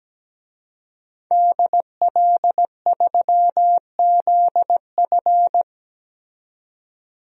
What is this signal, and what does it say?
Morse code, heard as a keyed tone, about 17 wpm, 700 Hz. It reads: DL3ZF
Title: DXCC-100 Call Signs - 5 Characters - 17wpm